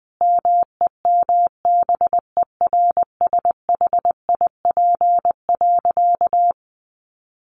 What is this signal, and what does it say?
Morse code, heard as a keyed tone, about 19 wpm, 700 Hz. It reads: MEMBERSHIP.
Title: Sentences from Top 1400 Words 20wpm